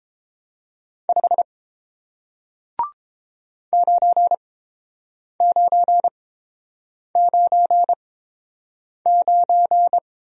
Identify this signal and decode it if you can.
Morse code, heard as a keyed tone, about 30 wpm, spaced as for 7 wpm, 700 Hz. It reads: 5 9999